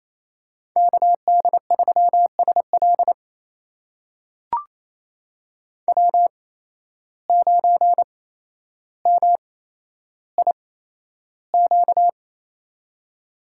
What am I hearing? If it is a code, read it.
Morse code, heard as a keyed tone, about 28 wpm, spaced as for 12 wpm, 700 Hz. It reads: KD3SL W9MIQ
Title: USA Call Signs - 5 Characters - 8x Character Spacing - 28wpm